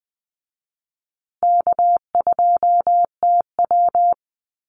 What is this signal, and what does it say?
Morse code, heard as a keyed tone, about 20 wpm, 700 Hz. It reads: K2TW
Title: Easy Call Signs Repeated 3x - 20wpm